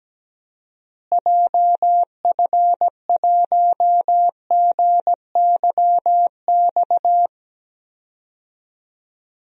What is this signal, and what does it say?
Morse code, heard as a keyed tone, about 17 wpm, 700 Hz. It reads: JF1GYX
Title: DXCC-100 Call Signs - 6 Characters - Repeated 3x - 17wpm